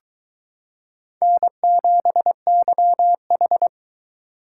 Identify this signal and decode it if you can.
Morse code, heard as a keyed tone, about 22 wpm, 700 Hz. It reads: N7YH